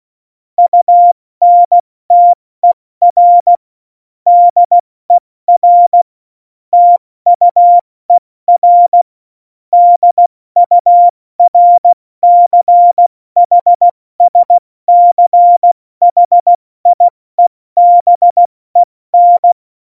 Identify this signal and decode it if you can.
Morse code, heard as a keyed tone, about 16 wpm, 700 Hz. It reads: UNTER DER TUER DURCHSCHIEBEN
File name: Morsekurs / Witze / Witz 02. Januar 2018, 70 F80